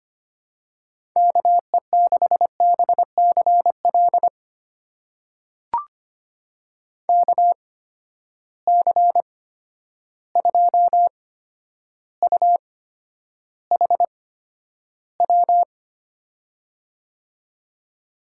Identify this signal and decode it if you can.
Morse code, heard as a keyed tone, about 25 wpm, spaced as for 11 wpm, 700 Hz. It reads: KE6BCL KC2UHW
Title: USA Call Signs - 6 Characters - 8x Character Spacing - 25wpm